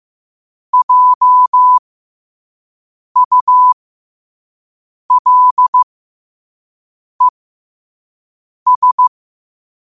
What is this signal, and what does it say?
Morse code, heard as a keyed tone, about 15 wpm, spaced as for 5 wpm, 1000 Hz. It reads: JULES
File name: qotdfiles / 5wpm